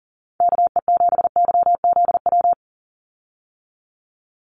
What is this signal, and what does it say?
Morse code, heard as a keyed tone, about 40 wpm, 700 Hz. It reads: KE7YZW
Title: USA Call Signs 40wpm